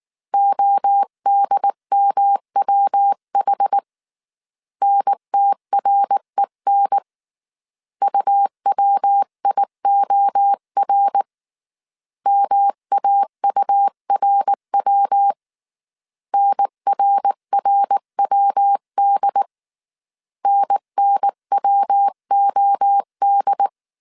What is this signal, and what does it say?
Morse code, heard as a keyed tone, about 19 wpm, 800 Hz. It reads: ODMWH NTREN UWIOR MAURW NRRWD NNWOD